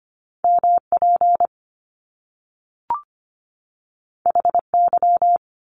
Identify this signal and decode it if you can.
Morse code, heard as a keyed tone, about 25 wpm, 700 Hz. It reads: M P H Y
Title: Top Two Letter Combinations in Words 25wpm